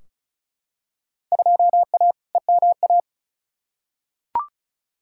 Morse code, heard as a keyed tone, about 36 wpm, 700 Hz. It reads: 2A EMA
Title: Field Day Exchanges Encore - 35wpm